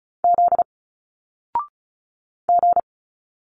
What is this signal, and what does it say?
Morse code, heard as a keyed tone, about 35 wpm, 700 Hz. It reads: Z G
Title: Single Letters 35wpm